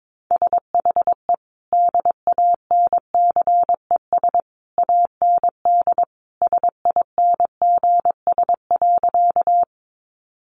Morse code, heard as a keyed tone, about 22 wpm, 700 Hz. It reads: SHE DANCES AND SINGS.
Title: Easy Phrases 22wpm